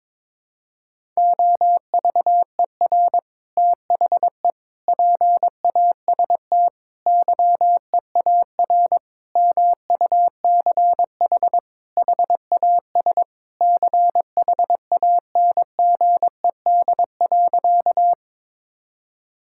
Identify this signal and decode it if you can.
Morse code, heard as a keyed tone, about 22 wpm, 700 Hz. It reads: OVER THE PAST YEAR MUCH HAS CHANGED.
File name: practice / 22wpm / Sentences with 3-Word Phrases - 22wpm